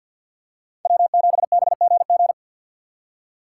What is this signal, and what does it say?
Morse code, heard as a keyed tone, about 53 wpm, 700 Hz. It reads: W7BGG
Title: USA Call Signs - 5 Characters - 6x Character Spacing - 50wpm